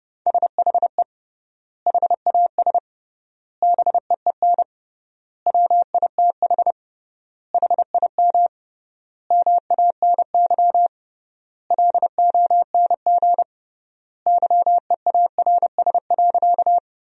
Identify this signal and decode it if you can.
Morse code, heard as a keyed tone, about 32 wpm, 700 Hz. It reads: SHE HAS BEEN WITH HIM MANY LONG YEARS.
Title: 3x Word-Spacing - Sentences from Top 200 Words - 30wpm